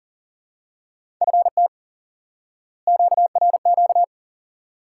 Morse code, heard as a keyed tone, about 40 wpm, 700 Hz. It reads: FT QRQ
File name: IQE Course 3 - Lesson 89 - YAGI - Two Elements - 40wpm